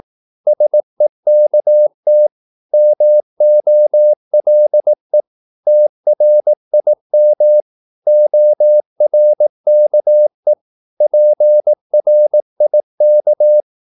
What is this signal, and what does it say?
Morse code, heard as a keyed tone, about 18 wpm, 600 Hz. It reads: SEKT MOLE TRIM ORKE PRIK
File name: Korte ord DK 18wpm